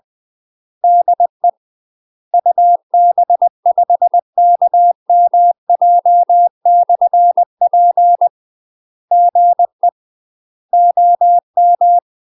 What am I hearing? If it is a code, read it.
Morse code, heard as a keyed tone, about 20 wpm, 700 Hz. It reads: DE UB5KMJ/P GE OM
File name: UK MASTER QSOs TRAINING/20WPM